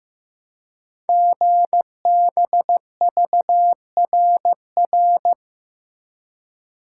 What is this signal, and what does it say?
Morse code, heard as a keyed tone, about 15 wpm, 700 Hz. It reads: GBVRR
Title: Code Groups - 5 Letters - 15wpm